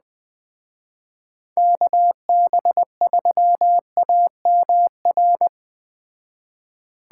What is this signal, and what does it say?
Morse code, heard as a keyed tone, about 20 wpm, 700 Hz. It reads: KB3AMR
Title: Speed Racing - USA 2x3 Call Signs 17wpm